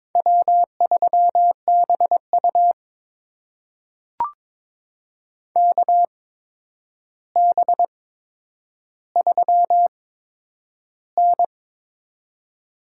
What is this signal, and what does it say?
Morse code, heard as a keyed tone, about 22 wpm, spaced as for 9 wpm, 700 Hz. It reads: W3BU KB3N
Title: USA Call Signs - 4 Characters - 8x Character Spacing - 22wpm